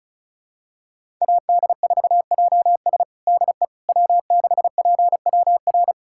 Morse code, heard as a keyed tone, about 34 wpm, 700 Hz. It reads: AD4JS DE W6PWR
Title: USA Callsign DE Callsign - 5 Characters - 35wpm